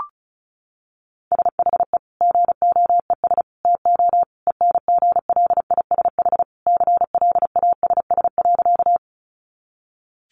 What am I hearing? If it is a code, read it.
Morse code, heard as a keyed tone, about 34 wpm, 700 Hz. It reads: SHE GOES TO ENGLISH CLASS.
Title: Additional Easy Phrases 35wpm